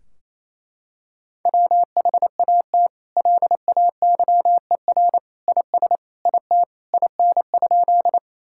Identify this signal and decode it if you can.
Morse code, heard as a keyed tone, about 28 wpm, 700 Hz. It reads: WHAT LAYER IS IT IN?
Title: Sentences from Top 1600 Words 28wpm